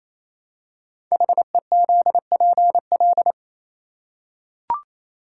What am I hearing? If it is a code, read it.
Morse code, heard as a keyed tone, about 28 wpm, 700 Hz. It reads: HEZPL
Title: Code Groups - 5 Letters - 28wpm